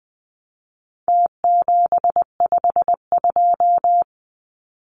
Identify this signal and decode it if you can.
Morse code, heard as a keyed tone, about 20 wpm, 700 Hz. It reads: T752
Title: Numbers - Contest Style - 4-Digit 20wpm